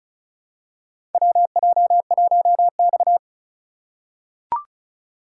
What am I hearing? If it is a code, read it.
Morse code, heard as a keyed tone, about 36 wpm, 700 Hz. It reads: WJ1X
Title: USA Call Signs - 4 Characters - 3x Character Spacing - 35wpm